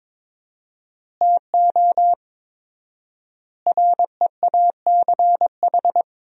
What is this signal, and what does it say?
Morse code, heard as a keyed tone, about 22 wpm, 700 Hz. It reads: TO REACH
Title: 4x Word-Spacing - Sentences from Top 400 Words - 22wpm